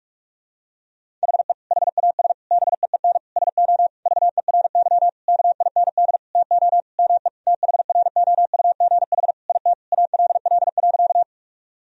Morse code, heard as a keyed tone, about 46 wpm, 700 Hz. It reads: HE HAS BEEN SO VERY KIND TO ME THROUGH IT ALL.
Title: Sentences from Top 1000 Words - Review Part 1 - 45wpm